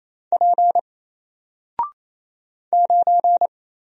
Morse code, heard as a keyed tone, about 27 wpm, 700 Hz. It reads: P 9
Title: Single Character (Letter:Number:Punctuation:Prosign) 28wpm